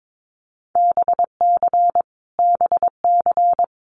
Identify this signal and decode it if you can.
Morse code, heard as a keyed tone, about 22 wpm, 700 Hz. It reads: BC BC
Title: Canadian Provinces and Territories - Repeated 3x - 22wpm